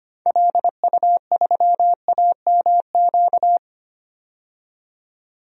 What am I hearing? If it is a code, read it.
Morse code, heard as a keyed tone, about 25 wpm, 700 Hz. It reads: LU3AMQ
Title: DXCC-100 Call Signs - 6 Characters - 25wpm